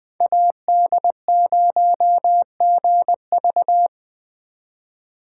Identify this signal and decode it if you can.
Morse code, heard as a keyed tone, about 20 wpm, 700 Hz. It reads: AD0GV
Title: Speed Racing - USA 2x2 Call Signs 17wpm